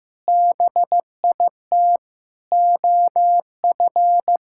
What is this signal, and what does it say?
Morse code, heard as a keyed tone, about 15 wpm, 700 Hz. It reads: BIT OF
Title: Sentences from Top 1000 Words - Review Part 5 - 15wpm